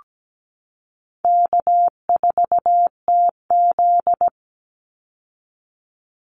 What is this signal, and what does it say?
Morse code, heard as a keyed tone, about 17 wpm, 700 Hz. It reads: K4TZ
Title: Easy Call Signs Repeated 3x - 17wpm